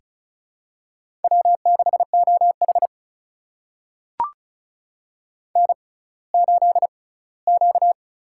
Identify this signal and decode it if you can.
Morse code, heard as a keyed tone, about 36 wpm, spaced as for 20 wpm, 700 Hz. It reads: W6OH N8Q
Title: USA Call Signs - 4 Characters - 6x Character Spacing - 35wpm